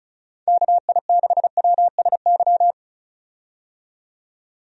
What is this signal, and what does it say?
Morse code, heard as a keyed tone, about 36 wpm, 700 Hz. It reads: KI6WSY
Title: USA Call Signs - 6 Characters - 35wpm